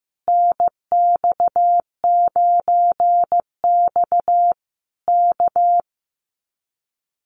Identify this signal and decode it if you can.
Morse code, heard as a keyed tone, about 15 wpm, 700 Hz. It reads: NX9X K
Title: Common QSO Phrases Encore 15wpm